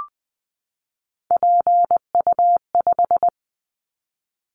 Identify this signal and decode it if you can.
Morse code, heard as a keyed tone, about 20 wpm, 700 Hz. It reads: PU5
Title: Callsign Prefixes for Top 50 DXCC - 20wpm